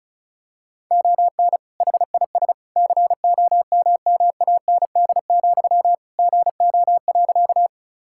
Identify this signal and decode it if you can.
Morse code, heard as a keyed tone, about 34 wpm, 700 Hz. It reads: ON HIS COMMAND, GO.
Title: Sentences from Top 1600 Words 35wpm